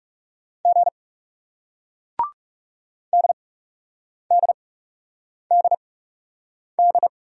Morse code, heard as a keyed tone, about 40 wpm, spaced as for 8 wpm, 700 Hz. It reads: G DDDD